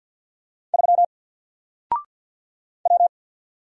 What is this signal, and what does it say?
Morse code, heard as a keyed tone, about 48 wpm, 700 Hz. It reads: 3 W
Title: Single Letter-Number Alt 50wpm